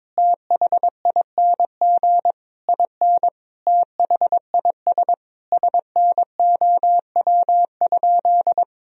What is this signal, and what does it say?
Morse code, heard as a keyed tone, about 23 wpm, 700 Hz. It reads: THING IN THIS SNOW?
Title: Sentences from Top 1000 Words - Review Part 3 - 22wpm